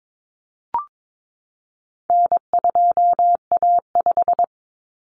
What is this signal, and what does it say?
Morse code, heard as a keyed tone, about 22 wpm, 700 Hz. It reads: N2A5
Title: Common Cut Numbers - 4-Digit 22wpm